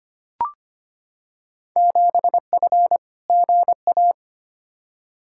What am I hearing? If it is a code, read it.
Morse code, heard as a keyed tone, about 25 wpm, 700 Hz. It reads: 7F GA
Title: Field Day Exchanges Encore - 25wpm